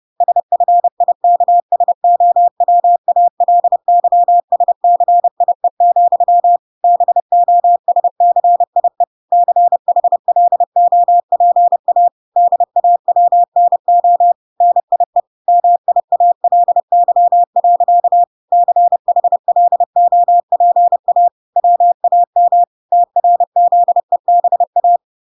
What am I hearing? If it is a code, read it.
Morse code, heard as a keyed tone, about 30 wpm, 700 Hz. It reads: SFIKSOWALYSCIE, BOSCIE CHLOPA DAWNO NIE MIALY. CHLOPA WAM TRZEBA